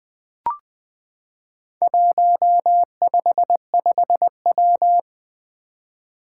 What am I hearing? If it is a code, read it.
Morse code, heard as a keyed tone, about 20 wpm, 700 Hz. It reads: 155W
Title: Four Challenging Letter-Number Combinations 20wpm